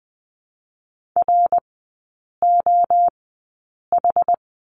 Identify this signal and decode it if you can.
Morse code, heard as a keyed tone, about 20 wpm, spaced as for 8 wpm, 700 Hz. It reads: ROH
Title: 093 Lesson 16e - Letter c - Three Characters 20wpm